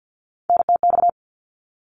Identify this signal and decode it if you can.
Morse code, heard as a keyed tone, about 50 wpm, 700 Hz. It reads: NTX
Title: ARRL Sections 50wpm